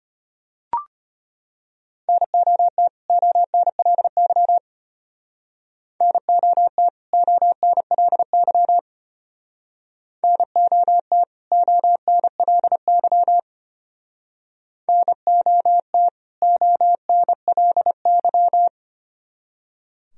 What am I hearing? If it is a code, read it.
Morse code, heard as a keyed tone, about 32 wpm, 700 Hz. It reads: NOT ONLY NOT ONLY NOT ONLY NOT ONLY